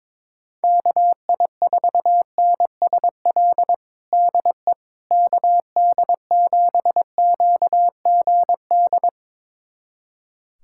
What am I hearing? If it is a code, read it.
Morse code, heard as a keyed tone, about 22 wpm, 700 Hz. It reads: KI4NSL DE KD7QGD
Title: USA Callsign DE Callsign - 6 Characters - 22wpm